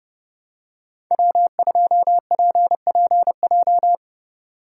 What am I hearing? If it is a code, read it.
Morse code, heard as a keyed tone, about 32 wpm, 700 Hz. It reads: W2PPJ
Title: USA Call Signs - 5 Characters - 4x Character Spacing - 30wpm